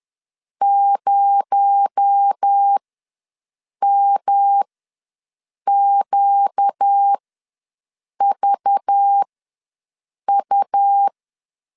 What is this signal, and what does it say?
Morse code, heard as a keyed tone, about 10 wpm, spaced as for 6 wpm, 800 Hz. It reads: 0MQVU
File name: LEZIONI / 7 GF9Z / 7 G ws 10wpm 3843